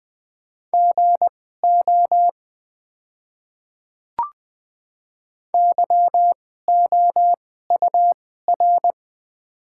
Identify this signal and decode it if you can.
Morse code, heard as a keyed tone, about 20 wpm, spaced as for 15 wpm, 700 Hz. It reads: GO YOUR
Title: 2x Character Spacing - Top 100 Words - 20wpm